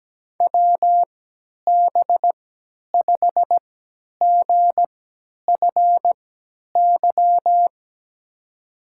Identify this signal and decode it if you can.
Morse code, heard as a keyed tone, about 17 wpm, spaced as for 10 wpm, 700 Hz. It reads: WB5GFY
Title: USA Call Signs - 3x Character Spacing - 17wpm